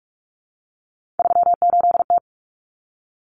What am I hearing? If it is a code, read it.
Morse code, heard as a keyed tone, about 44 wpm, 700 Hz. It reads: 38T
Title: Common Cut Numbers - 3-Digit 45wpm